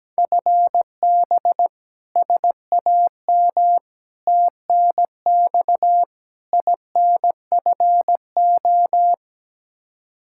Morse code, heard as a keyed tone, about 17 wpm, 700 Hz. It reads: FB SAM TNX INFO
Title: Practice QSO - Exchange 3 - More Info - 17wpm